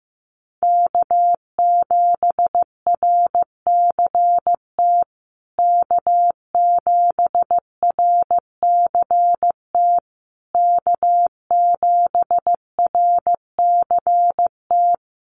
Morse code, heard as a keyed tone, about 15 wpm, 700 Hz. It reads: K7RCT K7RCT K7RCT
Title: USA Call Signs Repeated 3x 15wpm